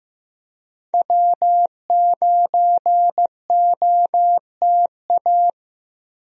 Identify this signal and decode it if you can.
Morse code, heard as a keyed tone, about 15 wpm, 700 Hz. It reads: W9OTA
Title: USA Call Signs - 5 Characters - 15wpm